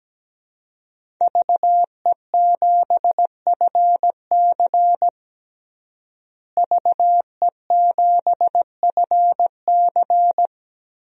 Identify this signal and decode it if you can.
Morse code, heard as a keyed tone, about 17 wpm, 700 Hz. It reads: VE7FC VE7FC
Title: DXCC-100 Call Signs - 5 Characters - Repeated 2x - 17wpm